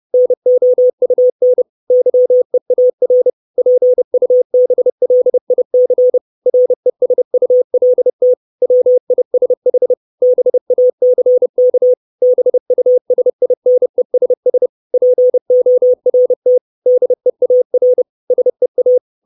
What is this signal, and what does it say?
Morse code, heard as a keyed tone, about 30 wpm, 500 Hz. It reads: NOUN YEAR PUBLIC RESULT WISH BACK BUSINESS PORT DEAR SEA
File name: 632 Words 30WPM